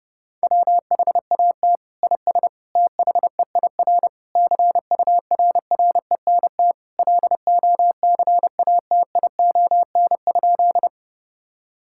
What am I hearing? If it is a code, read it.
Morse code, heard as a keyed tone, about 32 wpm, 700 Hz. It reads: WHAT IS THEIR CURRENT LOCATION?
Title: Sentences from Top 1000 Words - Review Part 6 - 30wpm